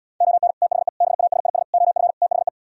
Morse code, heard as a keyed tone, about 77 wpm, 690 Hz. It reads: ON THE BASIS OF THE